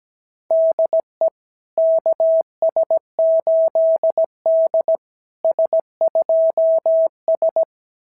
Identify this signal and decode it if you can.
Morse code, heard as a keyed tone, about 17 wpm, 660 Hz. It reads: DE KS8D S2S
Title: SOTA QSO Practice 17wpm